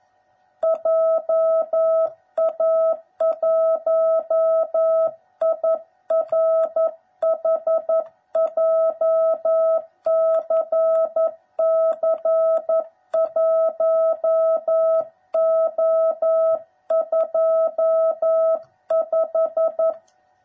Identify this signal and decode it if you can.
Morse code, heard as a keyed tone, about 11 wpm, 660 Hz. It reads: JA1IRH JCC1O25
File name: jh8cbh / 2021cw / cw09b